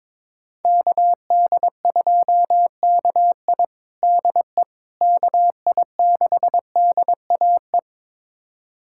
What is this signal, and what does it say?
Morse code, heard as a keyed tone, about 22 wpm, 700 Hz. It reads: KD2KI DE KI6DAE
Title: USA Callsign DE Callsign 22wpm